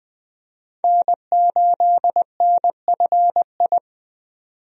Morse code, heard as a keyed tone, about 20 wpm, 700 Hz. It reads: N8NFI
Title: USA Call Signs - 5 Characters - 20wpm